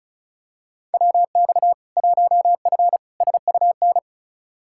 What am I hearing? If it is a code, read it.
Morse code, heard as a keyed tone, about 34 wpm, 700 Hz. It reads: WX 1F SUN
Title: Practice QSO - Exchange 2 - Info - 35wpm